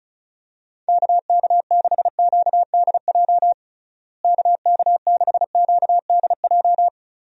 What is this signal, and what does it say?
Morse code, heard as a keyed tone, about 36 wpm, 700 Hz. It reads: KK6QDJ KK6QDJ
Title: USA Call Signs - 6 Characters - Repeated 3x - 35wpm